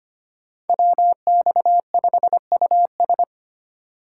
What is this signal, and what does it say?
Morse code, heard as a keyed tone, about 33 wpm, 700 Hz. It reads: WX5US